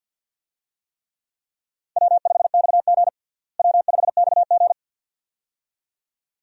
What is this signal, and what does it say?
Morse code, heard as a keyed tone, about 49 wpm, 700 Hz. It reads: W5XG W5XG
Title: USA Call Signs - 4 Characters - Repeated 2x - 50wpm